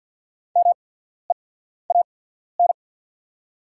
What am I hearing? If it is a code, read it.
Morse code, heard as a keyed tone, about 48 wpm, spaced as for 14 wpm, 690 Hz. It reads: MEAN
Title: 8x Character Spacing - Top 200 Words - 50wpm